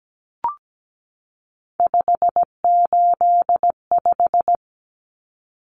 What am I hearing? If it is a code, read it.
Morse code, heard as a keyed tone, about 17 wpm, 700 Hz. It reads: 585
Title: Common Cut Numbers - 3-Digit 17wpm